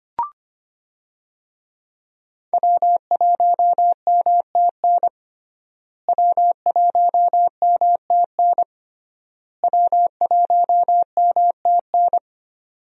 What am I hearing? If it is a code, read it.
Morse code, heard as a keyed tone, about 25 wpm, 700 Hz. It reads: W1MTN W1MTN W1MTN
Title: USA Call Signs - 5 Characters - Repeated 3x - 25wpm